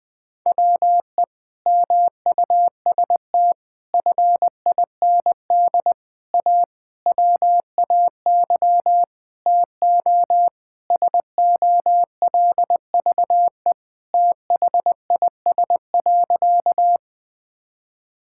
Morse code, heard as a keyed tone, about 19 wpm, 700 Hz. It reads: WE MUST FIND A WAY TO SOLVE THIS.
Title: Sentences with 4-Word Phrases - 20wpm